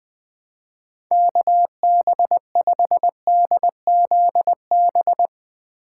Morse code, heard as a keyed tone, about 20 wpm, 700 Hz. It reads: KB5DZB
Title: Speed Racing - USA 2x3 Call Signs 17wpm